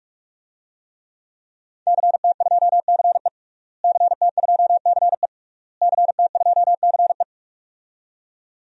Morse code, heard as a keyed tone, about 41 wpm, 700 Hz. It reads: CT2CE CT2CE CT2CE
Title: DXCC-100 Call Signs - 5 Characters - Repeated 3x - 45wpm